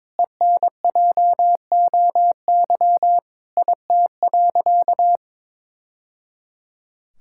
Morse code, heard as a keyed tone, about 22 wpm, 700 Hz. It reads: ENJOY IT.
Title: Sentences from Top 1000 Words - Review Part 5 - 22wpm